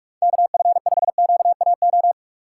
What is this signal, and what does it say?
Morse code, heard as a keyed tone, about 42 wpm, 700 Hz. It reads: KF5QAO